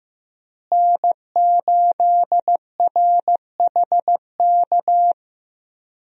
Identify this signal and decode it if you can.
Morse code, heard as a keyed tone, about 16 wpm, 700 Hz. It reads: N8RHK
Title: USA Call Signs - 5 Characters - Repeated 3x - 15wpm